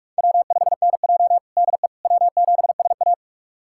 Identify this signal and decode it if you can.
Morse code, heard as a keyed tone, about 42 wpm, 700 Hz. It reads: W5NJ DE W7SA
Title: USA Callsign DE Callsign - 4 Characters - 45wpm